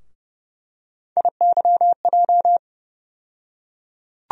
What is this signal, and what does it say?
Morse code, heard as a keyed tone, about 30 wpm, 700 Hz. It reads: IYJ
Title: USA Call Sign Postfixes - 3 Letters 30wpm